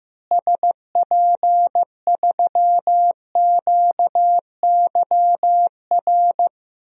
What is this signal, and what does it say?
Morse code, heard as a keyed tone, about 15 wpm, 700 Hz. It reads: SP3QYR
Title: DXCC-100 Call Signs - 6 Characters - Repeated 3x - 15wpm